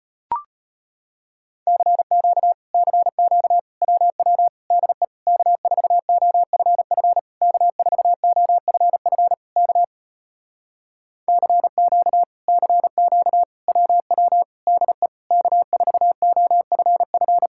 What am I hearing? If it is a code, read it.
Morse code, heard as a keyed tone, about 33 wpm, 700 Hz. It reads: CQ CQ WW DE K4OFF K4OFF K CQ CQ WW DE K4OFF